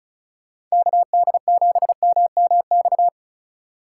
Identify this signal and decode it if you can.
Morse code, heard as a keyed tone, about 36 wpm, 700 Hz. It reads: KD7MMX
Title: USA Call Signs - 6 Characters - 2x Character Spacing - 35wpm